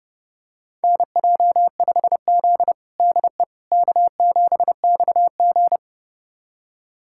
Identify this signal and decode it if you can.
Morse code, heard as a keyed tone, about 32 wpm, 700 Hz. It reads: NJ5Z DE K7XG
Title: USA Callsign DE Callsign - 4 Characters - 30wpm